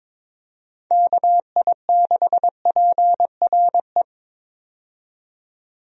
Morse code, heard as a keyed tone, about 22 wpm, 700 Hz. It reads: KI6PRE
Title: USA Call Signs - 6 Characters - 22wpm